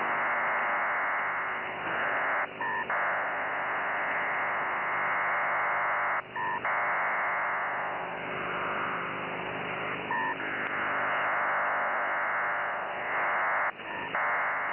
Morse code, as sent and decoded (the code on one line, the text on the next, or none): none